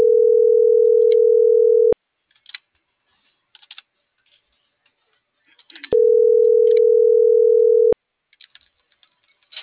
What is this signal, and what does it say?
Ringback tone (North America), 440 + 480 Hz.